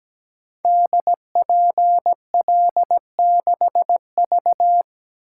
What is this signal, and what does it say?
Morse code, heard as a keyed tone, about 17 wpm, 700 Hz. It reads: DPL6V
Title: Code Groups - 5 Letter:Numbers - 17wpm